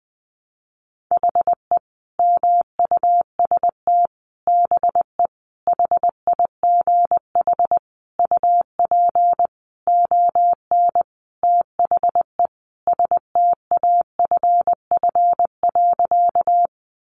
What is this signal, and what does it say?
Morse code, heard as a keyed tone, about 20 wpm, 700 Hz. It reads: HE MUST BE HIGH UP ON THE STAFF.
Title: Sentences from Top 1200 Words 20wpm